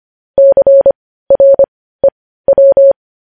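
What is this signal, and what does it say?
Morse code, heard as a keyed tone, about 25 wpm, spaced as for 15 wpm, 550 Hz. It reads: CREW
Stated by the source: Morse news headlines practice at 15.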